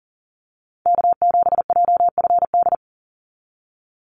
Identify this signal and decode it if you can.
Morse code, heard as a keyed tone, about 40 wpm, 700 Hz. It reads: K7JFD
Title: NA SOTA Chaser Callsigns 40wpm